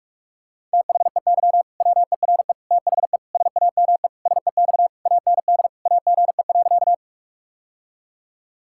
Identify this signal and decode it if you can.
Morse code, heard as a keyed tone, about 46 wpm, 700 Hz. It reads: THEY WERE THE SAME SEX AND AGE.
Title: Sentences from Top 1600 Words 45wpm